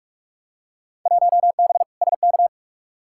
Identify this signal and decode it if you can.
Morse code, heard as a keyed tone, about 43 wpm, 700 Hz. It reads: 1B SK
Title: Field Day Exchanges Encore - 45wpm